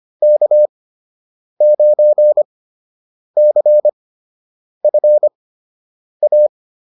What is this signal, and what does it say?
Morse code, heard as a keyed tone, about 25 wpm, spaced as for 8 wpm, 600 Hz. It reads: K9CFA